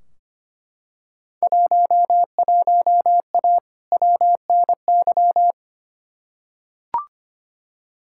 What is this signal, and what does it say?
Morse code, heard as a keyed tone, about 25 wpm, 700 Hz. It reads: 11A WNY
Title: Field Day Exchanges Encore - 25wpm